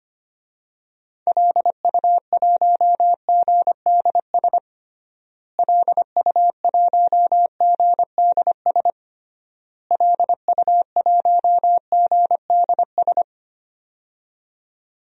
Morse code, heard as a keyed tone, about 25 wpm, 700 Hz. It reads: LU1GDS LU1GDS LU1GDS
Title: DXCC-100 Call Signs - 6 Characters - Repeated 3x - 25wpm